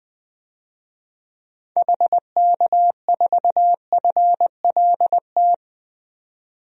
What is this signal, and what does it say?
Morse code, heard as a keyed tone, about 20 wpm, 700 Hz. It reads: HK4FLT
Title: DXCC-100 Call Signs - 6 Characters - 20wpm